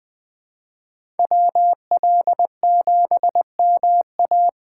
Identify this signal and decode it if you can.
Morse code, heard as a keyed tone, about 20 wpm, 700 Hz. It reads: WL7MA
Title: USA Call Signs - 5 Characters - 6x Character Spacing - 20wpm